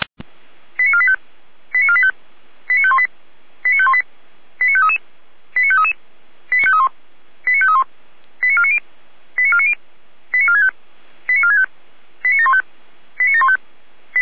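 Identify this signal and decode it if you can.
Five-tone selective call (ZVEI), heard as quick runs of five tones, tones of about 70 ms. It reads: 89485 89485 89518 89518 89533 89533 89531 89531 89490 89490 89465 89465 89715 89715
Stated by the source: ZVEI five-tone selective call 89436, received by SDR.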